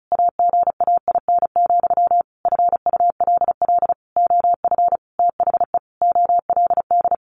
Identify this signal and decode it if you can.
Morse code, heard as a keyed tone, about 34 wpm, 700 Hz. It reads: AGAIN, FULL OF THE OLD
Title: A Sound Of Thunder 35wpm